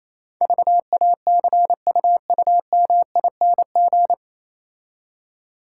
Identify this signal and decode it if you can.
Morse code, heard as a keyed tone, about 27 wpm, 700 Hz. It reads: VACUUMING
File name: Easy Words - Speed-Racing with Hints - 22wpm